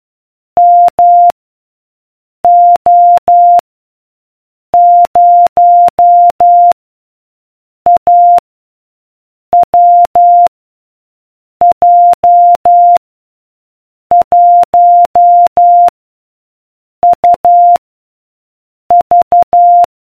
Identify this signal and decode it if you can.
Morse code, heard as a keyed tone, about 12 wpm, 700 Hz. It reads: M O 0 A W J 1 U V